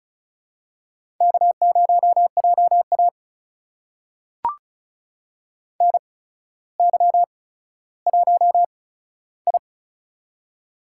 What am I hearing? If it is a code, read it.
Morse code, heard as a keyed tone, about 36 wpm, spaced as for 15 wpm, 700 Hz. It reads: K0JA NY1I
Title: USA Call Signs - 4 Characters - 8x Character Spacing - 35wpm